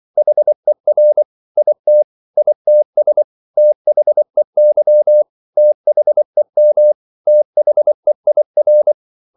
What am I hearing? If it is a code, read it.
Morse code, heard as a keyed tone, about 24 wpm, 600 Hz. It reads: HER IT ITS THEY THEM THEIR
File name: CW / Practice / 500MostCommonEnglishWords24